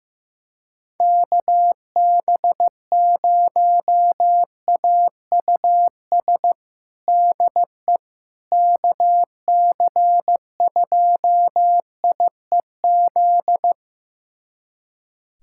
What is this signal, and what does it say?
Morse code, heard as a keyed tone, about 16 wpm, 700 Hz. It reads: KB0AUS DE KC2IEZ
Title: USA Callsign DE Callsign 15wpm